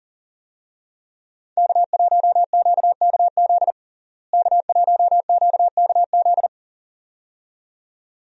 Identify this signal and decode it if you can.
Morse code, heard as a keyed tone, about 37 wpm, 700 Hz. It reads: K1QKZ K1QKZ
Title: USA Call Signs - 5 Characters - Repeated 2x - 40wpm